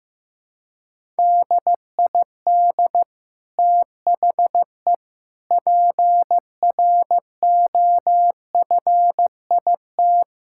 Morse code, heard as a keyed tone, about 15 wpm, 700 Hz. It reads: DID THE PROFIT
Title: Sentences from Top 1000 Words - Review Part 5 - 15wpm